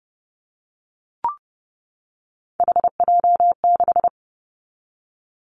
Morse code, heard as a keyed tone, about 30 wpm, 700 Hz. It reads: HJ6
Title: Callsign Prefixes for Top 100 DXCC - 30wpm